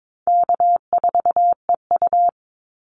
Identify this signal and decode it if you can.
Morse code, heard as a keyed tone, about 22 wpm, 700 Hz. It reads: K4EU
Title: Call Signs Repeated 3x - 22wpm